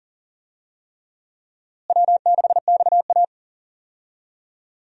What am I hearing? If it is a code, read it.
Morse code, heard as a keyed tone, about 37 wpm, 700 Hz. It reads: W6XA
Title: USA Call Signs - 4 Characters - Repeated 3x - 40wpm